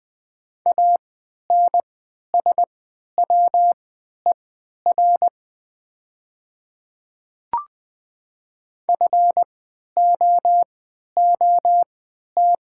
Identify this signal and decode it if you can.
Morse code, heard as a keyed tone, about 20 wpm, spaced as for 11 wpm, 700 Hz. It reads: ANSWER FOOT
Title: 3x Character Spacing - Top 500 Words - 20wpm